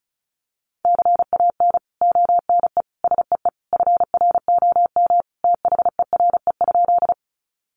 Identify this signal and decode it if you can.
Morse code, heard as a keyed tone, about 34 wpm, 700 Hz. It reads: CAN ONE SEE FROM THERE?
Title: Sentences from Top 100 Words 35wpm